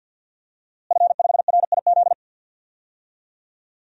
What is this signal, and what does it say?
Morse code, heard as a keyed tone, about 47 wpm, 700 Hz. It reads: F5RIZ
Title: DXCC-100 Call Signs - 5 Characters - Repeated 3x - 50wpm